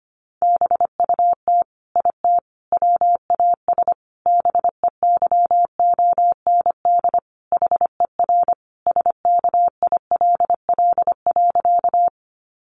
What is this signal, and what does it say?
Morse code, heard as a keyed tone, about 25 wpm, 700 Hz. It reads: BUT IT WAS BEYOND HER SKILL.
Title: Sentences from Top 1200 Words 25wpm